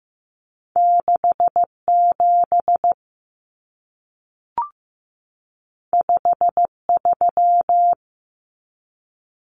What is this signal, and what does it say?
Morse code, heard as a keyed tone, about 15 wpm, 700 Hz. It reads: 67 53
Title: Common Cut Numbers - 2-Digit 15wpm